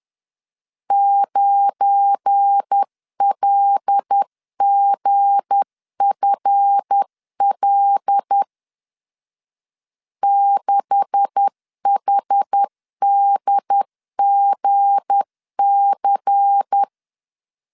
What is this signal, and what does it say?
Morse code, heard as a keyed tone, about 10 wpm, 800 Hz. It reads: 9LGFL 6HDGC
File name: LEZIONI / 9 286 10wpm 2735